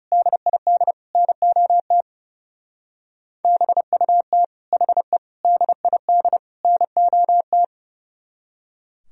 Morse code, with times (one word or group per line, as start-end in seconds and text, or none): 0.1-0.9 DID
1.2-2.0 NOT
3.4-4.4 BUT
4.7-5.2 HE
5.4-6.4 DID
6.6-7.6 NOT